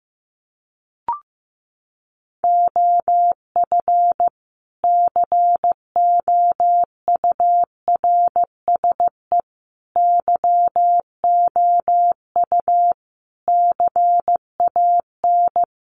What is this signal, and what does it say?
Morse code, heard as a keyed tone, about 15 wpm, 700 Hz. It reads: OF COURSE YOU CAN
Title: Sentences from Top 1300 Words 15wpm